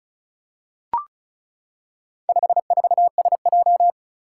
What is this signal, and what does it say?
Morse code, heard as a keyed tone, about 36 wpm, 700 Hz. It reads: 54SJ
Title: Four Challenging Letter-Number Combinations 35wpm